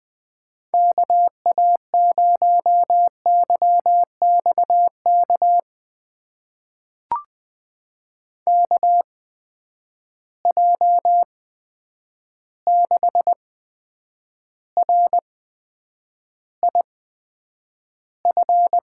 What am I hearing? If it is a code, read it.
Morse code, heard as a keyed tone, about 20 wpm, spaced as for 9 wpm, 700 Hz. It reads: KA0YXK KJ6RIF
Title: USA Call Signs - 6 Characters - 8x Character Spacing - 20wpm